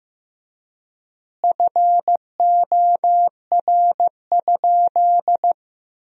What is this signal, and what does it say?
Morse code, heard as a keyed tone, about 15 wpm, 700 Hz. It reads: FOR?
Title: 3x Word-Spacing - Sentences from Top 200 Words - 15wpm